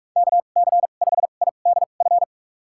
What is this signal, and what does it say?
Morse code, heard as a keyed tone, about 46 wpm, spaced as for 33 wpm, 700 Hz. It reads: KC5IDF
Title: USA Call Signs - 6 Characters - 2x Character Spacing - 45wpm